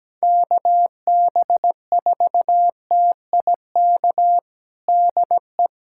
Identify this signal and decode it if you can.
Morse code, heard as a keyed tone, about 17 wpm, 700 Hz. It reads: KB4TIK DE
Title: USA Callsign DE Callsign - 6 Characters - 17wpm